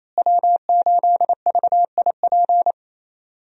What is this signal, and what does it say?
Morse code, heard as a keyed tone, about 28 wpm, 700 Hz. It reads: W8VIP
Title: USA Call Signs - 5 Characters - Repeated 2x - 28wpm